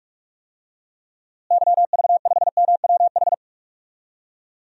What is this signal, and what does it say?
Morse code, heard as a keyed tone, about 46 wpm, 700 Hz. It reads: YV5MWH
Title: DXCC-100 Call Signs - 6 Characters - Repeated 2x - 45wpm